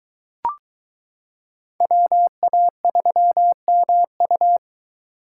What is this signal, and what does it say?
Morse code, heard as a keyed tone, about 22 wpm, 700 Hz. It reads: WA3MU